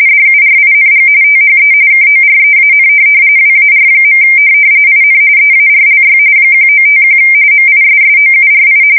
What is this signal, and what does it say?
RTTY, about 45.45 baud. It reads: DURING THE KOREAN WAR. IN EARLY 1951 MISSOURI
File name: fsk170 / USS-Missouri